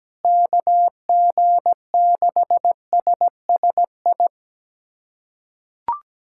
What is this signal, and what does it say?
Morse code, heard as a keyed tone, about 17 wpm, 700 Hz. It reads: KG6SSI
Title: USA Callsign DE Callsign - 6 Characters - 17wpm